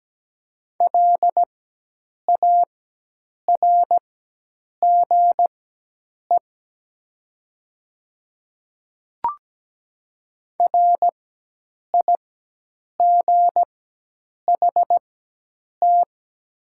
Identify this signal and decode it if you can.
Morse code, heard as a keyed tone, about 17 wpm, spaced as for 8 wpm, 700 Hz. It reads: LARGE RIGHT
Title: 4x Character Spacing - Top 200 Words - 17wpm